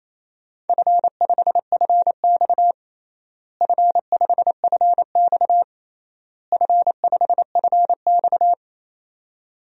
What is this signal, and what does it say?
Morse code, heard as a keyed tone, about 28 wpm, 700 Hz. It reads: F5FX F5FX F5FX
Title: DXCC-100 Call Signs - 4 Characters - Repeated 3x - 28wpm